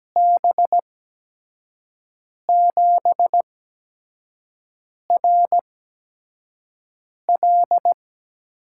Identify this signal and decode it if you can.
Morse code, heard as a keyed tone, about 17 wpm, spaced as for 5 wpm, 700 Hz. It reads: B7RL
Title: USA Call Signs - 6 Characters - 8x Character Spacing - 17wpm